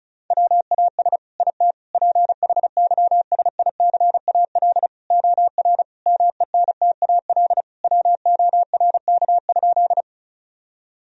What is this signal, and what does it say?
Morse code, heard as a keyed tone, about 36 wpm, 700 Hz. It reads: WAS IT PHYSICAL OR MENTAL WORK?
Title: Sentences from Top 1000 Words - Review Part 4 - 35wpm